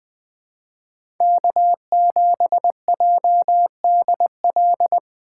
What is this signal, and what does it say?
Morse code, heard as a keyed tone, about 23 wpm, 700 Hz. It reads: K7JDL
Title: Speed Racing - USA 1x3 Call Signs 20wpm